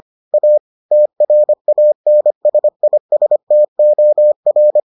Morse code, heard as a keyed tone, about 25 wpm, 600 Hz. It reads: A TRANSISTOR